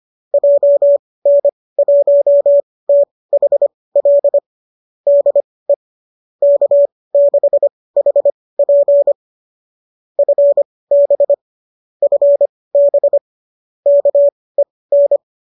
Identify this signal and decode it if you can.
Morse code, heard as a keyed tone, about 25 wpm, spaced as for 18 wpm, 570 Hz. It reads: JN1THL DE K6HP FB FB KEN
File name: qso205 18